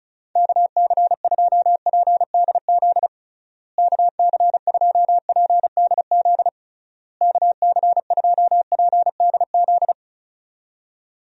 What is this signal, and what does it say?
Morse code, heard as a keyed tone, about 36 wpm, 700 Hz. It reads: KC2PDZ KC2PDZ KC2PDZ
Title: USA Call Signs - 6 Characters - Repeated 3x - 35wpm